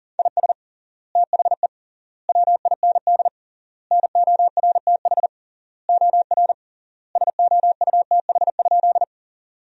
Morse code, heard as a keyed tone, about 40 wpm, 700 Hz. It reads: IS THE WIND NORTH OR SOUTH?
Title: 3x Word-Spacing - Sentences from Top 400 Words - 40wpm